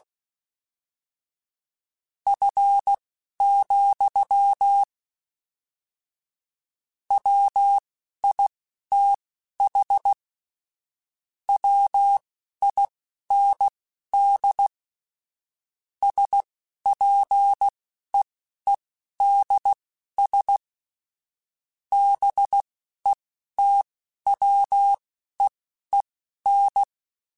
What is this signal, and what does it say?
Morse code, heard as a keyed tone, about 16 wpm, spaced as for 11 wpm, 800 Hz. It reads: F, WITH WIND SPEEDS BETWEEN
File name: Weather Sets / Weather Sets-mp3s / Weather 4 10 @10